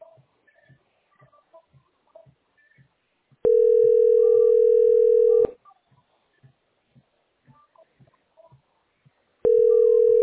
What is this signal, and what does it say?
Ringback tone (North America), 440 + 480 Hz.